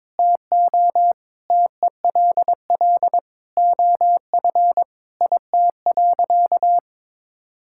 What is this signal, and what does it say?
Morse code, heard as a keyed tone, about 22 wpm, 700 Hz. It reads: TO TELL OF IT.
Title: Sentences from Top 1600 Words 22wpm